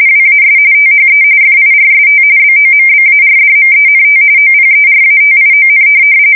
RTTY, about 45.45 baud. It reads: WAS THE FIRST BOMBARDMENT OPERATION